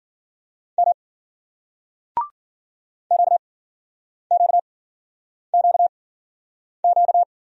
Morse code, heard as a keyed tone, about 55 wpm, spaced as for 9 wpm, 700 Hz. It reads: M QQQQ